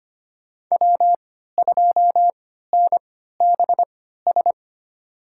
Morse code, heard as a keyed tone, about 25 wpm, spaced as for 14 wpm, 700 Hz. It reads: W2NBS
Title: USA Call Signs - 5 Characters - 3x Character Spacing - 25wpm